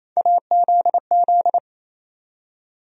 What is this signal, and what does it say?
Morse code, heard as a keyed tone, about 28 wpm, 700 Hz. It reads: AZZ
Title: USA Call Sign Postfixes - 3 Letters 28wpm